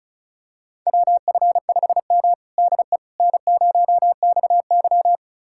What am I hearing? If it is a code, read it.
Morse code, heard as a keyed tone, about 36 wpm, 700 Hz. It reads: WF5M DE N0XY
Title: USA Callsign DE Callsign - 4 Characters - 35wpm